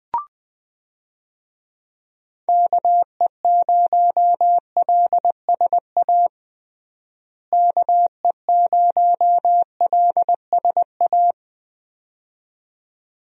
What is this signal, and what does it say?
Morse code, heard as a keyed tone, about 20 wpm, 700 Hz. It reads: KE0LSA KE0LSA
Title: USA Call Signs - 6 Characters - Repeated 2x - 20wpm